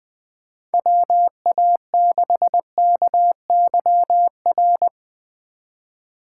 Morse code, heard as a keyed tone, about 20 wpm, 700 Hz. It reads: WA6KYR
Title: 2023 SOTA Chaser Call Signs - 20wpm